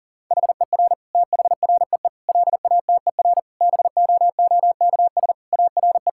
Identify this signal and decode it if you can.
Morse code, heard as a keyed tone, about 40 wpm, 700 Hz. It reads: HER THREE LATER BOOKS ARE
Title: Sentences from Top 1000 Words - Review Part 6 - 40wpm